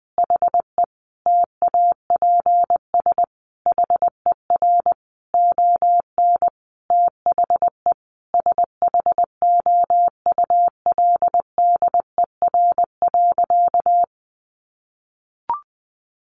Morse code, heard as a keyed tone, about 20 wpm, 700 Hz. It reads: HE TAPS HER ON THE SHOULDER.
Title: Additional Easy Phrases 20wpm